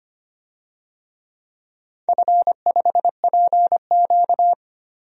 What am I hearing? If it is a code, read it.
Morse code, heard as a keyed tone, about 25 wpm, 700 Hz. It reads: F5PQ
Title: DXCC-100 Call Signs - 4 Characters - 25wpm